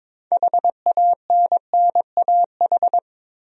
Morse code, heard as a keyed tone, about 22 wpm, 700 Hz. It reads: HANNAH
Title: USA Common Names - 22wpm